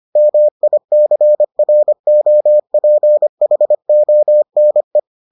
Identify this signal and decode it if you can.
Morse code, heard as a keyed tone, about 25 wpm, 600 Hz. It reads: MICROPHONE